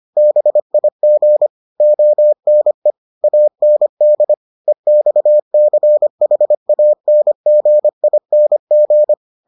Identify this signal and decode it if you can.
Morse code, heard as a keyed tone, about 25 wpm, 600 Hz. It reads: BIG ONE AND EXCHANGING